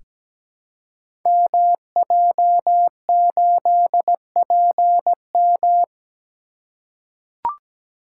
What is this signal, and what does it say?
Morse code, heard as a keyed tone, about 17 wpm, 700 Hz. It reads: MJ8PM
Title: Code Groups - 5 Letter:Numbers - 17wpm